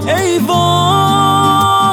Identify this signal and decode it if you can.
There is no code.